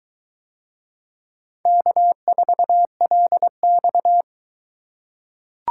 Morse code, heard as a keyed tone, about 21 wpm, 700 Hz. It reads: K4LX